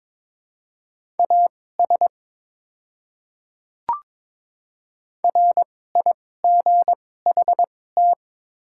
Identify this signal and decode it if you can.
Morse code, heard as a keyed tone, about 22 wpm, spaced as for 16 wpm, 700 Hz. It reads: AS RIGHT
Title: 2x Character Spacing - Top 200 Words - 22wpm